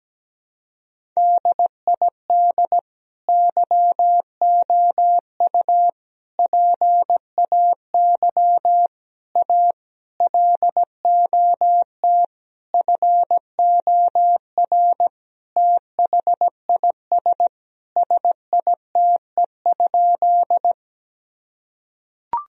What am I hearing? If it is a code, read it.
Morse code, heard as a keyed tone, about 17 wpm, 700 Hz. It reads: DID YOU PAY A LOT FOR THIS SITE?
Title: Sentences from Top 1000 Words - Review Part 4 - 17wpm